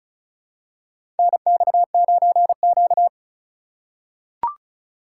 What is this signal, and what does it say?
Morse code, heard as a keyed tone, about 34 wpm, 700 Hz. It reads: NX9Q
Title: USA Call Signs - 4 Characters - 3x Character Spacing - 35wpm